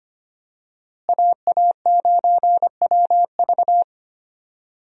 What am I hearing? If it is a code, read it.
Morse code, heard as a keyed tone, about 25 wpm, 700 Hz. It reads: AA9WV
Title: USA Call Signs - 5 Characters - Repeated 3x - 25wpm